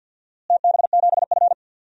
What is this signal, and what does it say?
Morse code, heard as a keyed tone, about 51 wpm, 700 Hz. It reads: TB7F
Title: DXCC-100 Call Signs - 4 Characters - Repeated 2x - 50wpm